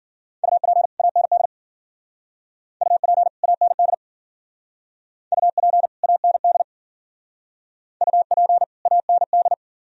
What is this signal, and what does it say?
Morse code, heard as a keyed tone, about 51 wpm, 700 Hz. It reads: UP AND UP AND UP AND UP AND